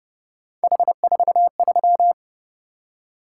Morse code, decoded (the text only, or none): H43